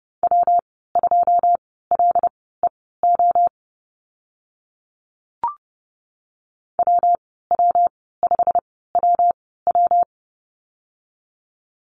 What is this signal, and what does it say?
Morse code, heard as a keyed tone, about 30 wpm, spaced as for 17 wpm, 700 Hz. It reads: W2LEO WW5WW
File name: Lesson 19g - Letter w - Call Signs 30wpm